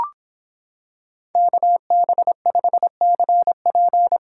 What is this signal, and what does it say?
Morse code, heard as a keyed tone, about 26 wpm, 700 Hz. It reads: KB5CP